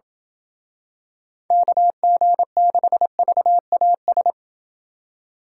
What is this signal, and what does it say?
Morse code, heard as a keyed tone, about 27 wpm, 700 Hz. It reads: KG6VAS